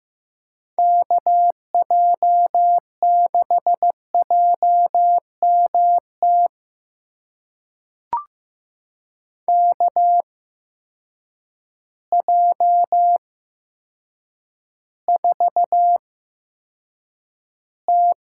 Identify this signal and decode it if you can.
Morse code, heard as a keyed tone, about 15 wpm, spaced as for 8 wpm, 700 Hz. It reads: KJ6JMT KJ4T
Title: USA Call Signs - 6 Characters - 8x Character Spacing - 15wpm